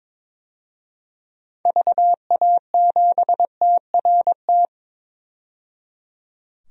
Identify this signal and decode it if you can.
Morse code, heard as a keyed tone, about 21 wpm, 700 Hz. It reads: VA7TRT
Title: DXCC-100 Call Signs - 6 Characters - 22wpm